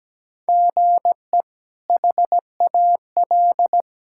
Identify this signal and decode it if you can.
Morse code, heard as a keyed tone, about 17 wpm, 700 Hz. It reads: GE HAL
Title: K1USN SST QSO Practice - 17wpm